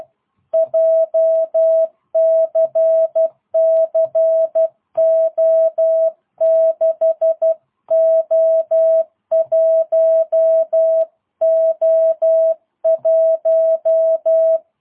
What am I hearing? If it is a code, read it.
Morse code, heard as a keyed tone, about 12 wpm, 650 Hz. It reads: JCCO6O1O1
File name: jh8cbh / 2022cw